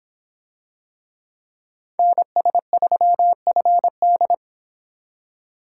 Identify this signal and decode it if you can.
Morse code, heard as a keyed tone, about 26 wpm, 700 Hz. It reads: NS3FD